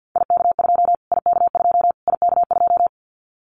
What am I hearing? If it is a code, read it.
Morse code, heard as a keyed tone, about 50 wpm, 700 Hz. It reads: IK2 IK2 IK2
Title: Callsign Prefixes for Top 25 DXCC - Repeated 3x - 50wpm